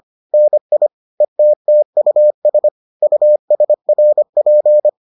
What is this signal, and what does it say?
Morse code, heard as a keyed tone, about 25 wpm, 600 Hz. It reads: NI ETTUS USRP